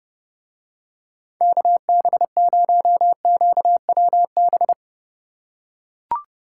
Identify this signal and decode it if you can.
Morse code, heard as a keyed tone, about 30 wpm, 700 Hz. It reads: KB0QWB